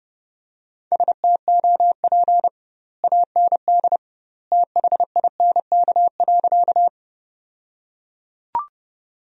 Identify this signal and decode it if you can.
Morse code, heard as a keyed tone, about 29 wpm, 700 Hz. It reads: STOP AND THINK.
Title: 2x Word-Spacing - Sentences from Top 300 Words - 30wpm